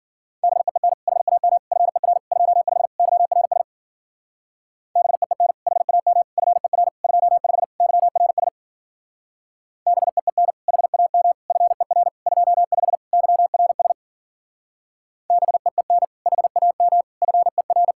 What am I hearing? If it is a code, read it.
Morse code, heard as a keyed tone, about 52 wpm, 700 Hz. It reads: BEEN HAM FER 25 YRS BEEN HAM FER 25 YRS BEEN HAM FER 25 YRS BEEN HAM FER